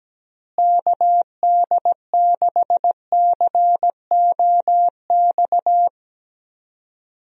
Morse code, heard as a keyed tone, about 17 wpm, 700 Hz. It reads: KD6COX
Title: USA Call Signs - 6 Characters - Repeated 2x - 17wpm